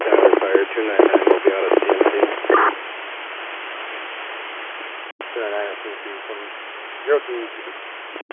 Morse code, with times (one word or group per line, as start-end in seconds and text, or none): none